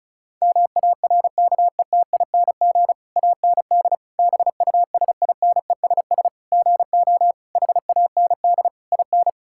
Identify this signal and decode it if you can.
Morse code, heard as a keyed tone, about 36 wpm, 690 Hz. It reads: MARKETING AND BUSINESS GO HAND IN
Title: Sentences from Top 1000 Words - Review Part 5 - 35wpm